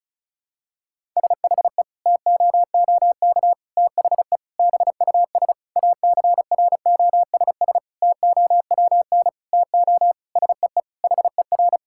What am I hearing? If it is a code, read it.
Morse code, heard as a keyed tone, about 36 wpm, 700 Hz. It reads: SHE TOOK THE BUS ACROSS TOWN TO SEE HER
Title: Sentences from Top 1000 Words - Review Part 6 - 35wpm